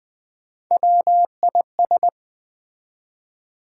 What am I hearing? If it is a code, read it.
Morse code, heard as a keyed tone, about 20 wpm, 700 Hz. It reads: WIS
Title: Build A Word - 1 Letter At A Time - 4-Letter Words - 20wpm